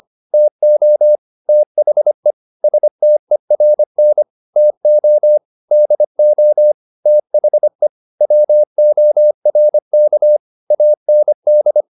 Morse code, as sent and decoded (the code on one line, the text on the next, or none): - --- / - .... / ... - . .-. -. / - --- / -.. --- / - .... / .-- --- .-. -.- / .- -. -..
TO THE STERN TO DO THE WORK AND